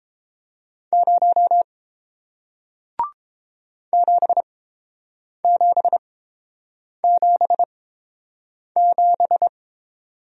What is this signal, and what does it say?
Morse code, heard as a keyed tone, about 30 wpm, spaced as for 7 wpm, 700 Hz. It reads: 0 7777